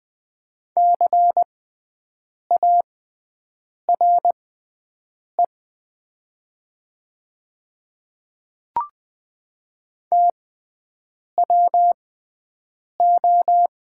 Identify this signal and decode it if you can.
Morse code, heard as a keyed tone, about 20 wpm, spaced as for 7 wpm, 700 Hz. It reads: CARE TWO
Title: 6x Character Spacing - Top 300 Words - 20wpm